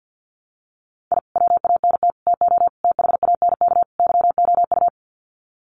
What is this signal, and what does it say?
Morse code, heard as a keyed tone, about 50 wpm, 700 Hz. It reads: I WANT TO THANK YOU
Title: Sets of 5 Words Encore 50wpm